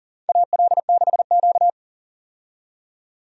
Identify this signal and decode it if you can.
Morse code, heard as a keyed tone, about 40 wpm, 700 Hz. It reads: AL6Q
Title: USA Call Signs - 4 Characters - 40wpm